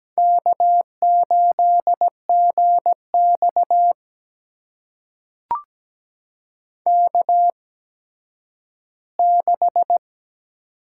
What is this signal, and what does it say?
Morse code, heard as a keyed tone, about 17 wpm, spaced as for 10 wpm, 700 Hz. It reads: K8GX K6
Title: USA Call Signs - 4 Characters - 8x Character Spacing - 17wpm